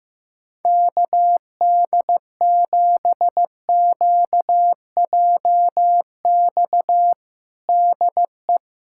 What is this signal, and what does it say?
Morse code, heard as a keyed tone, about 16 wpm, 700 Hz. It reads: KD7QJX DE
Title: USA Callsign DE Callsign 15wpm